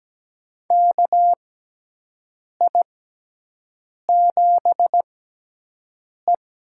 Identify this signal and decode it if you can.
Morse code, heard as a keyed tone, about 17 wpm, spaced as for 6 wpm, 700 Hz. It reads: KI7E
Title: USA Call Signs - 6x Character Spacing - 17wpm